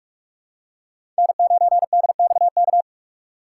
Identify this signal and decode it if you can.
Morse code, heard as a keyed tone, about 42 wpm, 700 Hz. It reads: N9DXK
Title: USA Call Signs - 5 Characters - 3x Character Spacing - 45wpm